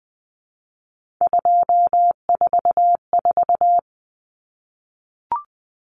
Morse code, heard as a keyed tone, about 20 wpm, 700 Hz. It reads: 244
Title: Numbers - 3-Digit 20wpm